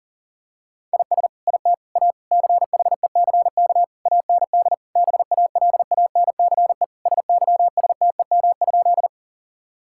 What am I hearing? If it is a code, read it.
Morse code, heard as a keyed tone, about 40 wpm, 700 Hz. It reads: IS IT A CHECK AND BALANCE SYSTEM?
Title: Sentences from Top 1000 Words - Review Part 6 - 40wpm